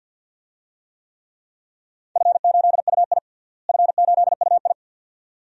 Morse code, heard as a keyed tone, about 53 wpm, 700 Hz. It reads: F8UI F8UI
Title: DXCC-100 Call Signs - 4 Characters - Repeated 2x - 50wpm